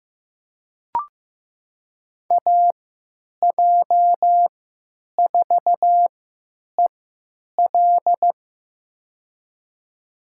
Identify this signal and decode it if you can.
Morse code, heard as a keyed tone, about 15 wpm, spaced as for 9 wpm, 700 Hz. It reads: AJ4EL
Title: USA Call Signs - 5 Characters - 3x Character Spacing - 15wpm